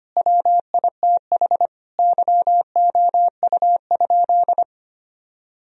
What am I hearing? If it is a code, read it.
Morse code, heard as a keyed tone, about 25 wpm, 700 Hz. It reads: WITH YOU?
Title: Sentences from Top 1000 Words - Review Part 1 - 25wpm